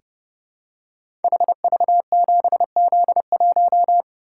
Morse code, heard as a keyed tone, about 30 wpm, 700 Hz. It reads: HV7Z1
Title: Code Groups - 5 Letter:Numbers - 30wpm